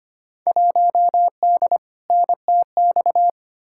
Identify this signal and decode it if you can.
Morse code, heard as a keyed tone, about 25 wpm, 700 Hz. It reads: 1D NTX
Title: Field Day Exchanges - 25wpm